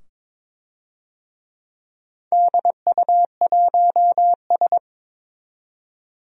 Morse code, heard as a keyed tone, about 22 wpm, 700 Hz. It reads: DU1S
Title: DXCC-100 Call Signs - 4 Characters - 22wpm